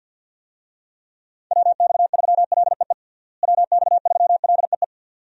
Morse code, heard as a keyed tone, about 52 wpm, 700 Hz. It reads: WX3LEE WX3LEE
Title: USA Call Signs - 6 Characters - Repeated 2x - 50wpm